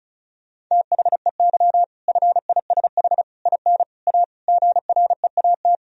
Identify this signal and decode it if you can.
Morse code, heard as a keyed tone, about 34 wpm, 700 Hz. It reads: THEY FISH IN A GREAT
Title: Sentences from Top 1000 Words - Review Part 1 - 35wpm